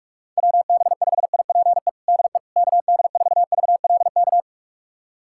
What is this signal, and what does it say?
Morse code, heard as a keyed tone, about 46 wpm, 700 Hz. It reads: WB5IPE DE KD4VLK
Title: USA Callsign DE Callsign 45wpm